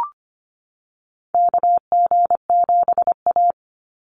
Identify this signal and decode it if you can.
Morse code, heard as a keyed tone, about 25 wpm, 700 Hz. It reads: KG7A
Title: Call Signs - 25wpm